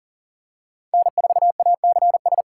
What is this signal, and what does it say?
Morse code, heard as a keyed tone, about 40 wpm, 700 Hz. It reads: N4ACS
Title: USA Call Signs - 5 Characters - 3x Character Spacing - 40wpm